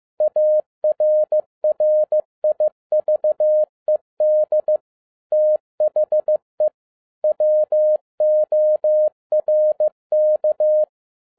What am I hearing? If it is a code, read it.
Morse code, heard as a keyed tone, about 15 wpm, 610 Hz. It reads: ARRIVED THE WORK
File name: FIELD DAY CONTEST QRV FROM SAKAE VILLAGE 15wpm